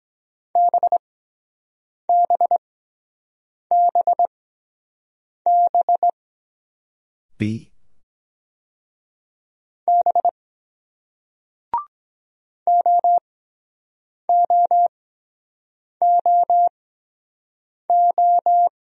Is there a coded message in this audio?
Yes: Morse code from 0.6 to 18.7 s, heard as a keyed tone, about 23 wpm, spaced as for 7 wpm, 700 Hz. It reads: BBBB B OOOO